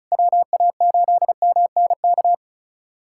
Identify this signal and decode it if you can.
Morse code, heard as a keyed tone, about 36 wpm, 700 Hz. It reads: WA8MNK
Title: USA Callsign DE Callsign 35wpm